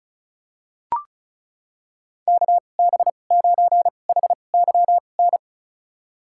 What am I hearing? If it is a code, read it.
Morse code, heard as a keyed tone, about 36 wpm, spaced as for 25 wpm, 700 Hz. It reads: KB9HYN
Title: USA Call Signs - 6 Characters - 2x Character Spacing - 35wpm